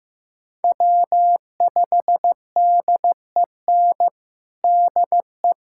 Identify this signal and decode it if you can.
Morse code, heard as a keyed tone, about 15 wpm, 700 Hz. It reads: W5DEN DE
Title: USA Callsign DE Callsign 15wpm